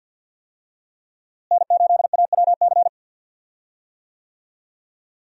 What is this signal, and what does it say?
Morse code, heard as a keyed tone, about 47 wpm, 700 Hz. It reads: N8AWC
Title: USA Call Signs - 5 Characters - Repeated 2x - 50wpm